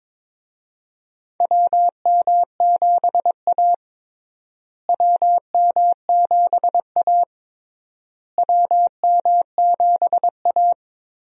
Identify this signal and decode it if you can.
Morse code, heard as a keyed tone, about 22 wpm, 700 Hz. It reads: WM7A WM7A WM7A
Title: USA Call Signs - 4 Characters - Repeated 3x - 22wpm